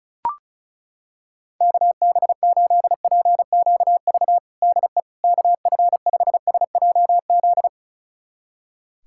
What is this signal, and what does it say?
Morse code, heard as a keyed tone, about 36 wpm, 700 Hz. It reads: KB8PQV DE KF5SJZ
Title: USA Callsign DE Callsign - 6 Characters - 35wpm